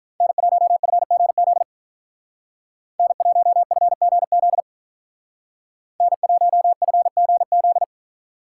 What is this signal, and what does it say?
Morse code, heard as a keyed tone, about 45 wpm, 700 Hz. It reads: N1FGZ N1FGZ N1FGZ